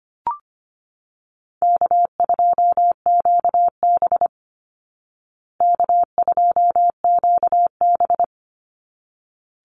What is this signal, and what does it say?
Morse code, heard as a keyed tone, about 25 wpm, 700 Hz. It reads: K2QB K2QB
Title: Call Signs Repeated 3x - 25wpm